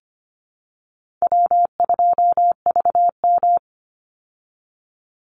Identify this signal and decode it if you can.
Morse code, heard as a keyed tone, about 25 wpm, 700 Hz. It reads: W2VM
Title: Easy Call Signs Repeated 3x - 25wpm